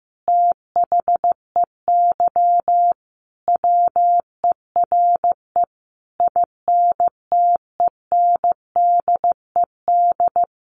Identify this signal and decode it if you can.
Morse code, heard as a keyed tone, about 15 wpm, 700 Hz. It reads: THEY WERE INTENDED
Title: A Sound Of Thunder 15wpm